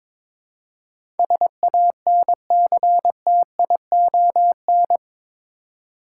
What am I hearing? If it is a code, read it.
Morse code, heard as a keyed tone, about 22 wpm, 700 Hz. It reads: SANCTION
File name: practice / 22wpm / Eight Letter Words - 22wpm